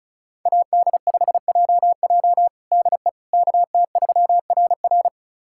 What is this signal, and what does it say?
Morse code, heard as a keyed tone, about 36 wpm, 700 Hz. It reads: AD5JJ DE KT3RR
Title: USA Callsign DE Callsign - 5 Characters - 35wpm